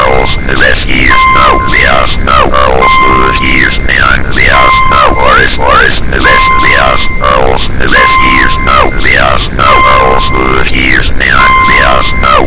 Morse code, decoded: none